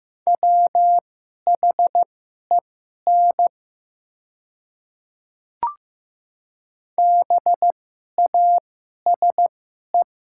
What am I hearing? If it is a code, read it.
Morse code, heard as a keyed tone, about 15 wpm, spaced as for 11 wpm, 700 Hz. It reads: WHEN BASE
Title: 2x Character Spacing - Top 500 Words - 15wpm